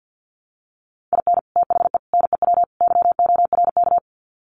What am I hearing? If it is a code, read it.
Morse code, heard as a keyed tone, about 50 wpm, 700 Hz. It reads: IN THE NEW YORK
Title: Sets of 4 Words 50wpm